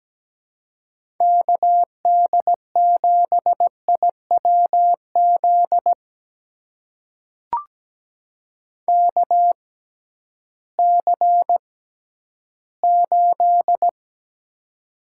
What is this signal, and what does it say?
Morse code, heard as a keyed tone, about 17 wpm, spaced as for 11 wpm, 700 Hz. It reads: KD7IWZ KC8
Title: USA Call Signs - 6 Characters - 6x Character Spacing - 17wpm